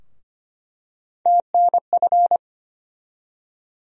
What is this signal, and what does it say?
Morse code, heard as a keyed tone, about 25 wpm, 700 Hz. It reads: TNF
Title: USA Call Sign Postfixes - 3 Letters 25wpm